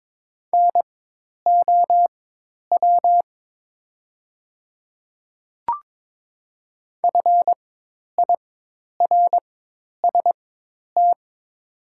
Morse code, heard as a keyed tone, about 23 wpm, spaced as for 10 wpm, 700 Hz. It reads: NOW FIRST
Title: 4x Character Spacing - Top 100 Words - 22wpm